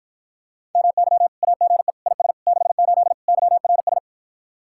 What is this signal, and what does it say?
Morse code, heard as a keyed tone, about 54 wpm, 700 Hz. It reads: MY AGE IS 68 YRS